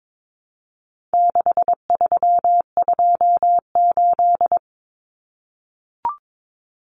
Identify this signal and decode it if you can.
Morse code, heard as a keyed tone, about 22 wpm, 700 Hz. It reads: 6328
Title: Common Cut Numbers - 4-Digit 22wpm